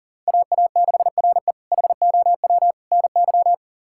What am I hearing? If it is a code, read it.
Morse code, heard as a keyed tone, about 40 wpm, 700 Hz. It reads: AA6RE HOW NY
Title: K1USN SST QSO Practice - 40wpm